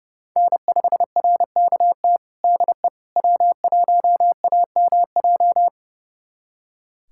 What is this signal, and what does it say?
Morse code, heard as a keyed tone, about 30 wpm, 700 Hz. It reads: N5RKT DE W1AMJ
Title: USA Callsign DE Callsign - 5 Characters - 30wpm